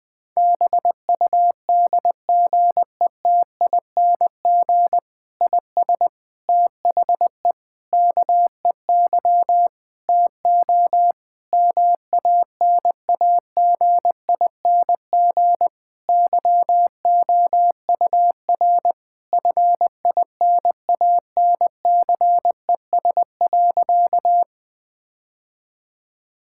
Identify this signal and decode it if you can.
Morse code, heard as a keyed tone, about 20 wpm, 700 Hz. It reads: BUDGETING IS THE KEY TO MANAGING YOUR FINANCES.
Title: Difficult Sentences with Hints - 20wpm